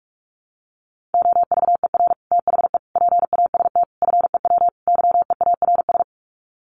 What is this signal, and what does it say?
Morse code, heard as a keyed tone, about 44 wpm, 700 Hz. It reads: OVER THE PAST FEW YEARS
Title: Sets of 5 Words Encore 45wpm